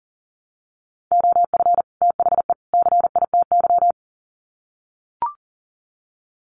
Morse code, heard as a keyed tone, about 40 wpm, 700 Hz. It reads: OF THE CITY
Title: Sets of 3 Words 40wpm